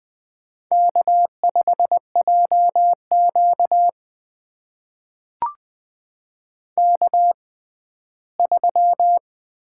Morse code, heard as a keyed tone, about 20 wpm, spaced as for 14 wpm, 700 Hz. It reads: K5JQ K3
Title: USA Call Signs - 4 Characters - 6x Character Spacing - 20wpm